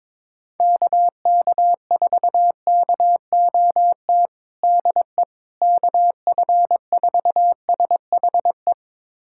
Morse code, heard as a keyed tone, about 22 wpm, 700 Hz. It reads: KK4KOT DE KF4SHE
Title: USA Callsign DE Callsign 22wpm